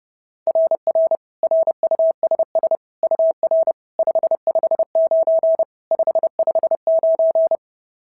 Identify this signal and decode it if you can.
Morse code, heard as a keyed tone, about 30 wpm, 690 Hz. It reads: RR RUSS UR 559 559
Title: SOTA QSO Practice 30wpm